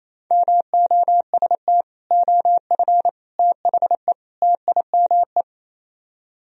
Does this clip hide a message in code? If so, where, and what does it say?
Yes: Morse code from 0.3 to 5.4 s, heard as a keyed tone, about 37 wpm, 700 Hz. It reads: MOST OF THE TIME